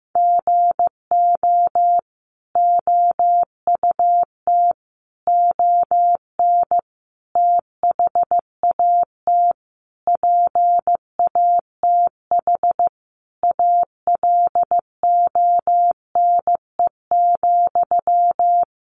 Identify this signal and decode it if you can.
Morse code, heard as a keyed tone, about 15 wpm, 700 Hz. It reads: GO OUT ON THAT PATH ALONE,
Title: A Sound Of Thunder 15wpm